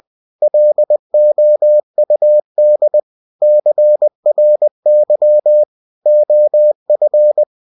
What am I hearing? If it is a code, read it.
Morse code, heard as a keyed tone, about 20 wpm, 600 Hz. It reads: LOUD CRY OF
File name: cw-books / winnie-the-pooh / winnie-20wpm-page-0094